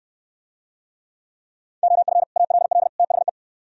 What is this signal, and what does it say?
Morse code, heard as a keyed tone, about 68 wpm, 700 Hz. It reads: OF ALL THE